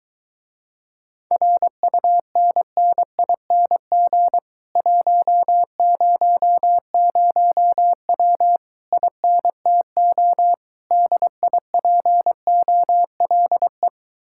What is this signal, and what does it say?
Morse code, heard as a keyed tone, about 23 wpm, 700 Hz. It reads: RUNNING 100W INTO DIPOLE